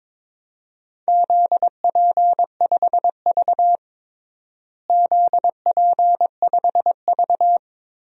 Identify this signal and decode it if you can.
Morse code, heard as a keyed tone, about 22 wpm, 700 Hz. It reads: ZP5V ZP5V
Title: DXCC-100 Call Signs - 4 Characters - Repeated 3x - 22wpm